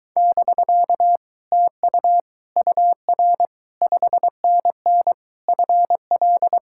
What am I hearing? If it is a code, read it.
Morse code, heard as a keyed tone, about 24 wpm, 700 Hz. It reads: <BK> TU UR 5NN FL